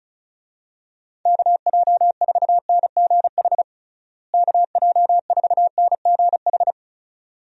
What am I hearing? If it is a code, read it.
Morse code, heard as a keyed tone, about 34 wpm, 700 Hz. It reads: KJ4NGH KJ4NGH
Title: USA Call Signs - 6 Characters - Repeated 2x - 35wpm